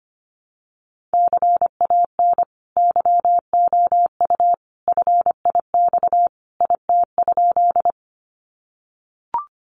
Morse code, heard as a keyed tone, about 25 wpm, 700 Hz. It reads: CAN YOU FIX IT?
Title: Sentences from Top 1400 Words 25wpm